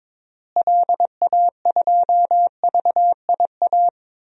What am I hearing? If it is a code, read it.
Morse code, heard as a keyed tone, about 22 wpm, 700 Hz. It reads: LA2VIA
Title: DXCC-100 Call Signs - 6 Characters - Repeated 3x - 22wpm